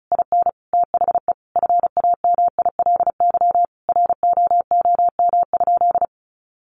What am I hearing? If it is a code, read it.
Morse code, heard as a keyed tone, about 34 wpm, 700 Hz. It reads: IN THE FAMILY ROOM?
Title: Sentences from Top 1200 Words 35wpm